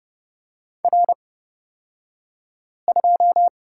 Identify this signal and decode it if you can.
Morse code, heard as a keyed tone, about 31 wpm, spaced as for 5 wpm, 700 Hz. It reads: R2
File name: Lesson 37d - Letter x - Two Characters 30wpm